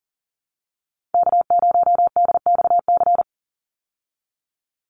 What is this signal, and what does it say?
Morse code, heard as a keyed tone, about 40 wpm, 700 Hz. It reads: K0DXC
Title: Call Signs Repeated 3x - 40wpm